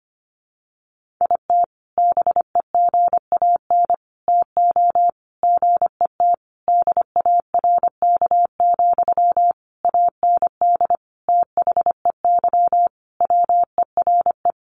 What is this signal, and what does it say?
Morse code, heard as a keyed tone, about 25 wpm, 700 Hz. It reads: IT BEGAN TO GET DARK, AND THEY WERE
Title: Sentences from Top 1400 Words 25wpm